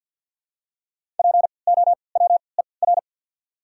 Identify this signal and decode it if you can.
Morse code, heard as a keyed tone, about 47 wpm, spaced as for 28 wpm, 700 Hz. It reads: POWER
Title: 3x Character Spacing - Top 500 Words - 50wpm